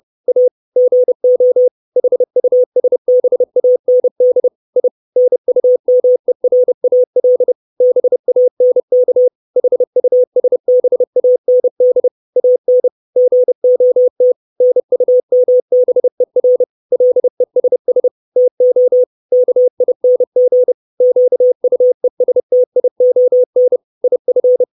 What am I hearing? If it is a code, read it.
Morse code, heard as a keyed tone, about 30 wpm, 500 Hz. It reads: A GO HUSBAND I NUMERAL BANK HUSBAND AN GOT NUMBER LESS TO KING QUESTION IF
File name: 632 words / 632 Words 30WPM